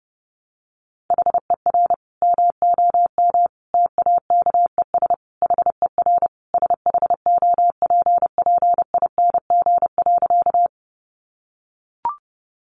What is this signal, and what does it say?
Morse code, heard as a keyed tone, about 30 wpm, 700 Hz. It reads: HER MOM TAKES HER SHOPPING.
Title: Additional Easy Phrases 30wpm